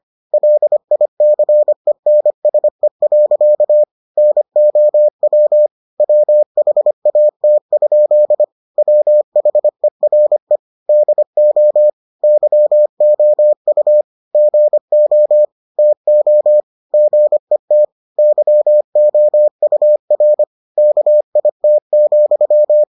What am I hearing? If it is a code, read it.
Morse code, heard as a keyed tone, about 25 wpm, 600 Hz. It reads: LICENSE. NOW WHAT? WHERE DO YOU GO TO GET YOUR KIT,